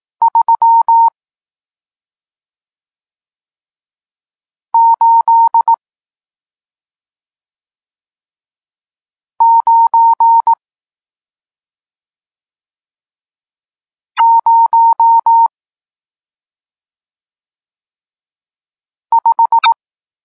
Morse code, heard as a keyed tone, about 18 wpm, 920 Hz. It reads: 3 8 9 0 5
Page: Tekst 40 – Nauka znaków przestankowych